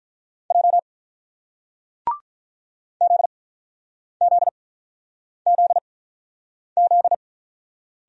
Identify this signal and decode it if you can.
Morse code, heard as a keyed tone, about 45 wpm, spaced as for 9 wpm, 700 Hz. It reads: J ZZZZ